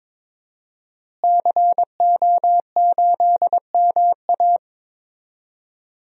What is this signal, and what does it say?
Morse code, heard as a keyed tone, about 22 wpm, 700 Hz. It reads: CO8MA
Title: DXCC-100 Call Signs - 5 Characters - Repeated 2x - 22wpm